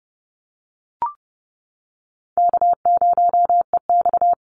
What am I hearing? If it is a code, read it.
Morse code, heard as a keyed tone, about 30 wpm, 700 Hz. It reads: K0EX
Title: Easy Call Signs - 30wpm